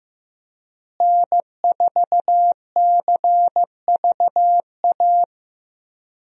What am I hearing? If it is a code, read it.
Morse code, heard as a keyed tone, about 15 wpm, 700 Hz. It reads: N4CVA
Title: USA Call Signs - 5 Characters - Repeated 3x - 15wpm